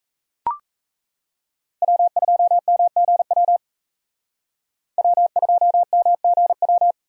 Morse code, heard as a keyed tone, about 40 wpm, 700 Hz. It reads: W2MGW W2MGW